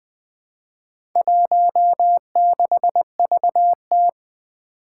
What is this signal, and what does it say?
Morse code, heard as a keyed tone, about 20 wpm, 700 Hz. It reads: 16VT
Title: Four Challenging Letter-Number Combinations 20wpm